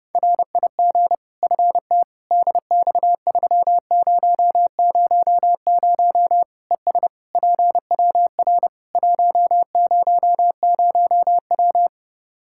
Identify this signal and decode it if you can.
Morse code, heard as a keyed tone, about 32 wpm, 700 Hz. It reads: RIG FT DX3000 ES PWR 100W
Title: Practice QSO - Exchange 2 - Info - 30wpm